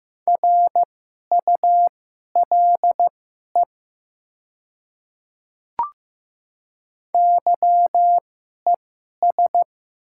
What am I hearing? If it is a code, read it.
Morse code, heard as a keyed tone, about 15 wpm, spaced as for 11 wpm, 700 Hz. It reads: RULE YES
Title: 2x Character Spacing - Top 500 Words - 15wpm